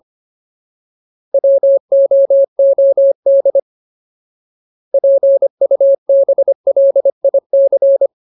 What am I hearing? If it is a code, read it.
Morse code, heard as a keyed tone, about 25 wpm, 550 Hz. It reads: WOOD PUBLIC
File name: Ultimate Word List / 25WPM Ultimate